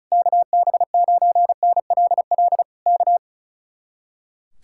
Morse code, heard as a keyed tone, about 33 wpm, 700 Hz. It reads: KB9NLL K
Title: Speed-Racing Common QSO Phrases - 35wpm